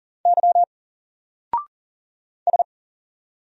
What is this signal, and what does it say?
Morse code, heard as a keyed tone, about 40 wpm, 700 Hz. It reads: Y S
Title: Single Letter-Number 40wpm